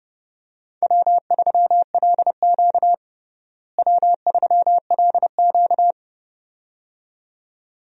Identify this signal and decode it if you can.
Morse code, heard as a keyed tone, about 30 wpm, 700 Hz. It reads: W3LQ W3LQ
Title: USA Call Signs - 4 Characters - Repeated 2x - 30wpm